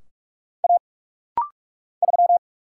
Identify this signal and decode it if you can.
Morse code, heard as a keyed tone, about 46 wpm, 700 Hz. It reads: A 3
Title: Single Letter-Number - Rapid-Fire 45wpm